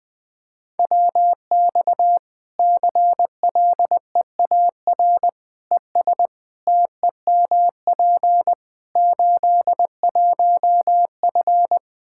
Morse code, heard as a keyed tone, about 20 wpm, 700 Hz. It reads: WX CLEAR ES TEMP 81F
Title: Practice QSO - Exchange 2 - Info - 20wpm